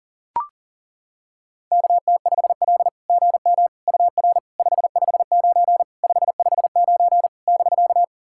Morse code, heard as a keyed tone, about 40 wpm, 700 Hz. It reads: KT5L GM UR 559 559 <BK>
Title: SOTA QSO Practice 40wpm